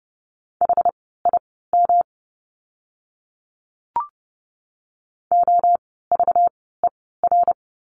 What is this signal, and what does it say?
Morse code, heard as a keyed tone, about 30 wpm, spaced as for 17 wpm, 700 Hz. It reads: HIM OVER
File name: course / 30:10wpm / 237 Lesson 38f - Letter q - Words 30wpm